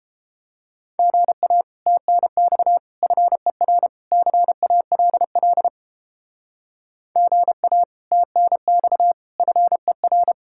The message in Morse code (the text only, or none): GA TNX FER CALL GA TNX FER